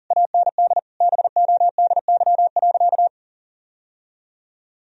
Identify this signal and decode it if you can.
Morse code, heard as a keyed tone, about 40 wpm, 700 Hz. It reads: AND BODY.
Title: Sentences from Top 1000 Words - Review Part 2 - 40wpm